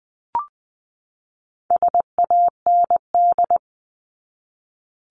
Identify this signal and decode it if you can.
Morse code, heard as a keyed tone, about 20 wpm, 700 Hz. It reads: SAND
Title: 260 Lesson 42e - Words 20-20wpm